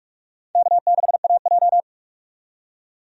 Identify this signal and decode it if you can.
Morse code, heard as a keyed tone, about 41 wpm, 700 Hz. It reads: K6AJ
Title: USA Call Signs - 4 Characters - 45wpm